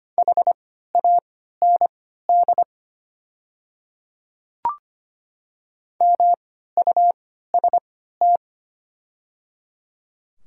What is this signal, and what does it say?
Morse code, heard as a keyed tone, about 25 wpm, spaced as for 14 wpm, 690 Hz. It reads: HAND MUST
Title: 3x Character Spacing - Top 200 Words - 25wpm